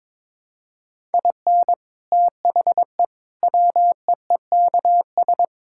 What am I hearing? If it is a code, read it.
Morse code, heard as a keyed tone, about 22 wpm, 700 Hz. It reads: IN THE WEEKS
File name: practice / 22wpm / Sets of 7 Words - 22wpm